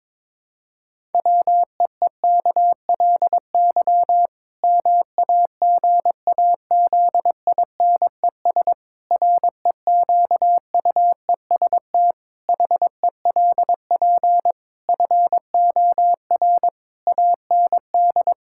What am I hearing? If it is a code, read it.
Morse code, heard as a keyed tone, about 22 wpm, 700 Hz. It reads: WEEKLY MAGAZINES REQUEST HELP FOR AND
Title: Pangram Challenge v2 - 22wpm